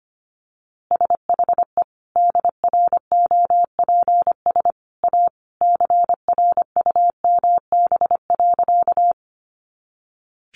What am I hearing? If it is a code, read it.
Morse code, heard as a keyed tone, about 25 wpm, 700 Hz. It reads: SHE DROPS A CRUMB.
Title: Easy Phrases 25wpm